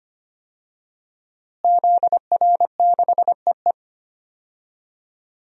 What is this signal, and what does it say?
Morse code, heard as a keyed tone, about 25 wpm, 700 Hz. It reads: ZR6EE
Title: DXCC-100 Call Signs - 5 Characters - Repeated 2x - 25wpm